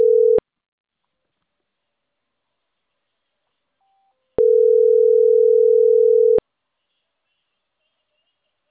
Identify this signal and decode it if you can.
Ringback tone (North America), 440 + 480 Hz.